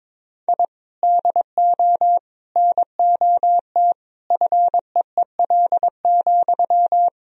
Morse code, heard as a keyed tone, about 22 wpm, 700 Hz. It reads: I DO NOT FEEL,
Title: Sentences from Top 1000 Words - Review Part 3 - 22wpm